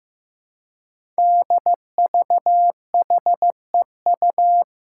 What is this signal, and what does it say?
Morse code, heard as a keyed tone, about 15 wpm, 700 Hz. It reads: DVHEU
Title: Five Challenging Letter-Number Combinations 15wpm